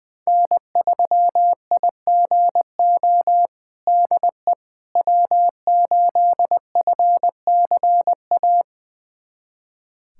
Morse code, heard as a keyed tone, about 20 wpm, 700 Hz. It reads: N3IGO DE W8FCA
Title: USA Callsign DE Callsign - 5 Characters - 20wpm